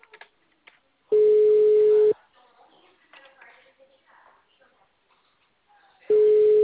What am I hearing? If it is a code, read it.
Ringback tone (Europe, 425 Hz), with speech.